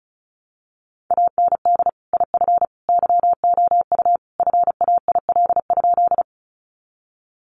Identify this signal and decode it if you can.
Morse code, heard as a keyed tone, about 34 wpm, 700 Hz. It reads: AND IF YOU FAIL?
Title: Sentences from Top 1100 Words 35wpm